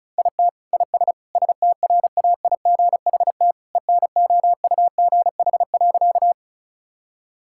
Morse code, heard as a keyed tone, about 36 wpm, 700 Hz. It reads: IT IS STRAIGHT ENOUGH.
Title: Sentences from Top 1000 Words - Review Part 6 - 35wpm